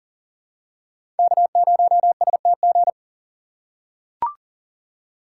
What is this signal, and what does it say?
Morse code, heard as a keyed tone, about 40 wpm, 700 Hz. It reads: K0STG
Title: USA Call Signs - 5 Characters - 6x Character Spacing - 40wpm